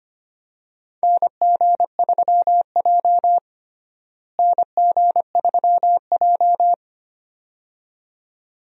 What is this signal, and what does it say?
Morse code, heard as a keyed tone, about 25 wpm, 700 Hz. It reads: NG3J NG3J
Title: USA Call Signs - 4 Characters - Repeated 2x - 25wpm